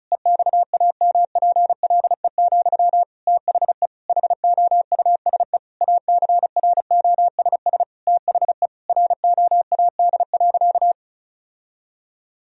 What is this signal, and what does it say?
Morse code, heard as a keyed tone, about 36 wpm, 700 Hz. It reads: EXAMPLE, THE HOUSE ACROSS THE ROAD.
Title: Sentences from Top 1000 Words - Review Part 1 - 35wpm